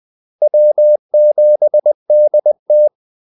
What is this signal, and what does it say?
Morse code, heard as a keyed tone, about 20 wpm, 600 Hz. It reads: W7DT
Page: Kaldesignaler 20 wpm